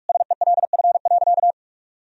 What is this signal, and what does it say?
Morse code, heard as a keyed tone, about 46 wpm, 700 Hz. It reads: SELF.
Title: Sentences from Top 1000 Words - Review Part 1 - 45wpm